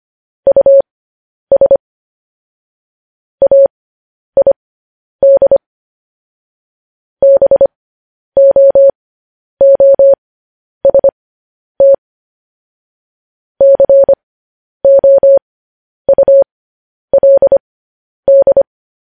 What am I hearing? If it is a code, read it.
Morse code, heard as a keyed tone, about 25 wpm, spaced as for 10 wpm, 550 Hz. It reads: US AID BOOST COULD
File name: podcast / News headlines-1x-25x10